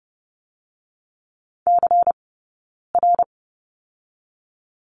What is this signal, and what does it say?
Morse code, heard as a keyed tone, about 30 wpm, 700 Hz. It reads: C R
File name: 092 Lesson 16d - Letter c - Two Characters 30wpm